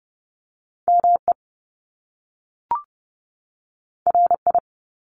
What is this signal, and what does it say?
Morse code, heard as a keyed tone, about 30 wpm, 700 Hz. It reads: ME RI
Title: Top Two Letter Combinations in Words 30wpm